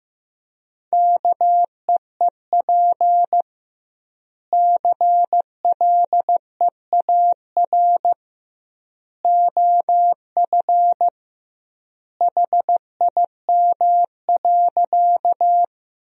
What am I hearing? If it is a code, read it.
Morse code, heard as a keyed tone, about 15 wpm, 700 Hz. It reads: KEEP CLEAR OF HIM.
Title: 2x Word-Spacing - Sentences from Top 500 Words - 15wpm